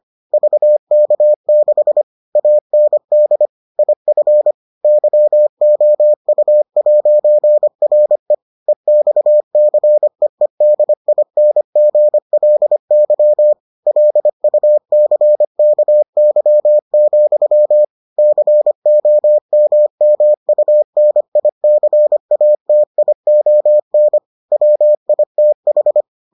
Morse code, heard as a keyed tone, about 25 wpm, 600 Hz. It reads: VK6 AND IF YOU'RE EXCEEDINGLY LUCKY, COMMUNICATION WITH